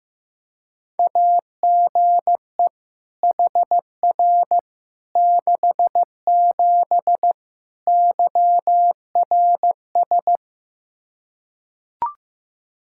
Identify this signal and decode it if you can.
Morse code, heard as a keyed tone, about 15 wpm, 700 Hz. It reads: AGE HR 67 YRS
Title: Practice QSO - Exchange 3 - More Info - 15wpm